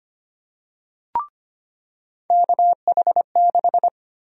Morse code, heard as a keyed tone, about 25 wpm, 700 Hz. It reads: KH6
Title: USA Call Sign Prefixes - 25wpm